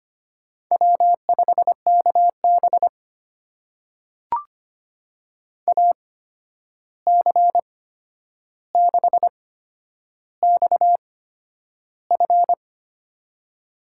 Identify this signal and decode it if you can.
Morse code, heard as a keyed tone, about 25 wpm, spaced as for 10 wpm, 700 Hz. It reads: W5KB AC6XF
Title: USA Call Signs - 8x Character Spacing - 25wpm